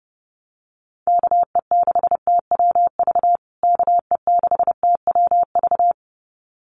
Morse code, heard as a keyed tone, about 30 wpm, 700 Hz. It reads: KE6TWV KE6TWV
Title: USA Call Signs Repeated 2x 30wpm